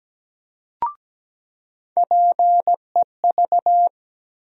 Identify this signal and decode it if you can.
Morse code, heard as a keyed tone, about 17 wpm, 700 Hz. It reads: PEV
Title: USA Call Sign Postfixes - 3 Letters 17wpm